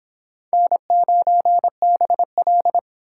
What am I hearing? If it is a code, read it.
Morse code, heard as a keyed tone, about 23 wpm, 700 Hz. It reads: N9BL